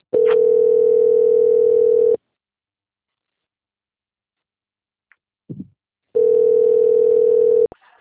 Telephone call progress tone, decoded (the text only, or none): ringback tone